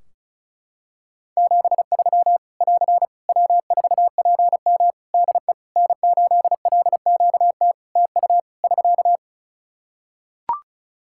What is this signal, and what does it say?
Morse code, heard as a keyed tone, about 34 wpm, 700 Hz. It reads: 73 <AR> W4PM DE N8LQT TU <SK>
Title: Practice QSO - Exchange 4 - Ending - 35wpm